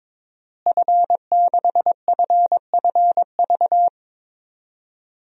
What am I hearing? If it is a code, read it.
Morse code, heard as a keyed tone, about 22 wpm, 700 Hz. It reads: F6FFV
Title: DXCC-100 Call Signs - 5 Characters - Repeated 2x - 22wpm